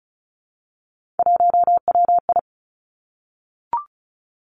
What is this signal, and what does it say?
Morse code, heard as a keyed tone, about 36 wpm, 700 Hz. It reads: 1WI
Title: Three Challenging Letter-Number Combinations 35wpm